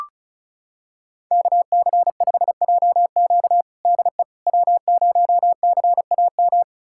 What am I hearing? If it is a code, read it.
Morse code, heard as a keyed tone, about 36 wpm, 700 Hz. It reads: KC5JQ DE W0CAM
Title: USA Callsign DE Callsign - 5 Characters - 35wpm